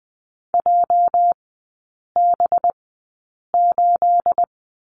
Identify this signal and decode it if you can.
Morse code, heard as a keyed tone, about 20 wpm, spaced as for 8 wpm, 700 Hz. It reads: JB8
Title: 212 Lesson 34e - Number 8 - Three Characters 20wpm